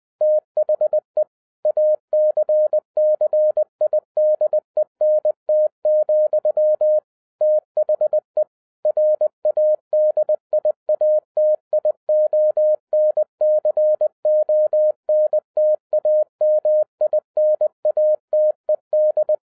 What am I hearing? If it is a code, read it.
Morse code, heard as a keyed tone, about 20 wpm, 610 Hz. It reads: THE ACCIDENT, THE RADIATIONCONTAMINATED